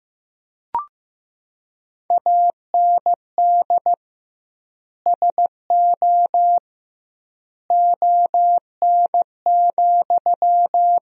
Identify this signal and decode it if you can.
Morse code, heard as a keyed tone, about 15 wpm, 700 Hz. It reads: AND SO ON,
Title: 2x Word-Spacing - Sentences from Top 100 Words - 15wpm